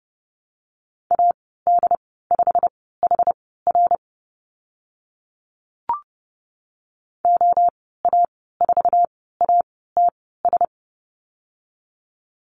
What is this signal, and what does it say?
Morse code, heard as a keyed tone, about 30 wpm, spaced as for 17 wpm, 700 Hz. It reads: AD5HR OA4ATS
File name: Lesson 15g - Number 5 - Call Signs 30wpm